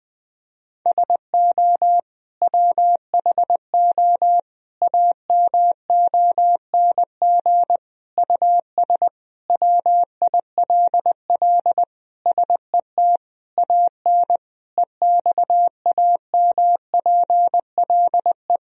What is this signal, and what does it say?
Morse code, heard as a keyed tone, about 19 wpm, 700 Hz. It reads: SO WHO AMONG US WILL SET AN EXAMPLE
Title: Sentences from Top 1000 Words - Review Part 3 - 20wpm